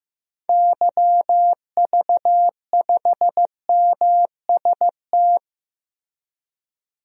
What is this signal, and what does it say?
Morse code, heard as a keyed tone, about 16 wpm, 700 Hz. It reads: YV5MST
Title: DXCC-100 Call Signs - 6 Characters - Repeated 2x - 15wpm